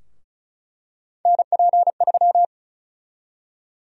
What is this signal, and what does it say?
Morse code, heard as a keyed tone, about 36 wpm, 700 Hz. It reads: NP3
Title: USA Call Sign Prefixes - Repeated 2x - 35wpm